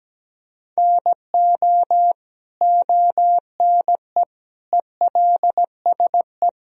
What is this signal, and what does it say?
Morse code, heard as a keyed tone, about 17 wpm, 700 Hz. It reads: NO ONE ELSE
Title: Sentences from Top 1000 Words - Review Part 3 - 17wpm